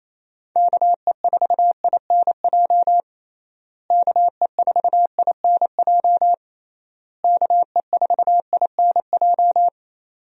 Morse code, heard as a keyed tone, about 27 wpm, 700 Hz. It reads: KE4INJ KE4INJ KE4INJ
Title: USA Call Signs - 6 Characters - Repeated 3x - 28wpm